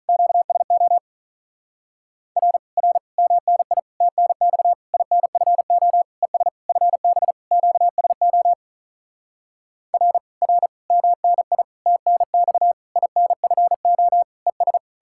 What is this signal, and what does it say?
Morse code, heard as a keyed tone, about 45 wpm, 700 Hz. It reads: QSO R R MNI TNX INFO ES FB QSO R R MNI TNX INFO ES